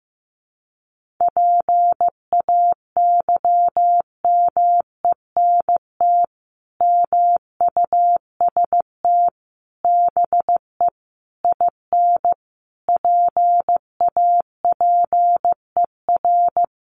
Morse code, heard as a keyed tone, about 15 wpm, 700 Hz. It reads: PAYMENT MUST BE IN PAPER
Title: Sentences from Top 1500 Words 15wpm